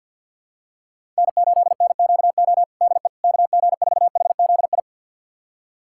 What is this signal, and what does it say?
Morse code, heard as a keyed tone, about 48 wpm, 700 Hz. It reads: N8NQO DE KG4HZI
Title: USA Callsign DE Callsign 50wpm